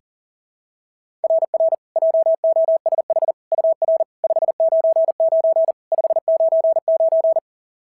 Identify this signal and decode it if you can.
Morse code, heard as a keyed tone, about 40 wpm, 690 Hz. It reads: RR JOSH UR 599 599
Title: SOTA QSO Practice 40wpm